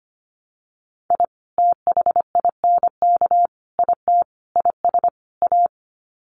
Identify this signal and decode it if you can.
Morse code, heard as a keyed tone, about 25 wpm, 700 Hz. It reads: I THINK IT IS A
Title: Sets of 5 Words Encore 25wpm